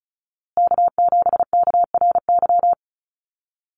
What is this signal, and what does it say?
Morse code, heard as a keyed tone, about 34 wpm, 700 Hz. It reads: K7KRY
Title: USA Call Signs Repeated 2x 35wpm